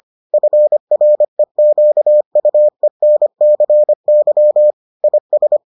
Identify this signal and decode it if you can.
Morse code, heard as a keyed tone, about 25 wpm, 600 Hz. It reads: FREQUENCY IS